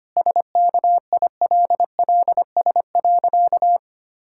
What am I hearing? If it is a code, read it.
Morse code, heard as a keyed tone, about 25 wpm, 700 Hz. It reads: SKILLS.
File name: practice / 25wpm / Difficult Sentences with Hints - 25wpm